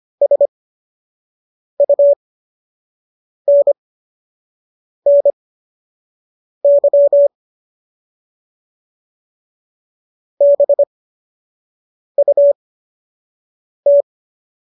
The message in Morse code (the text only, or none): SUNNY BUT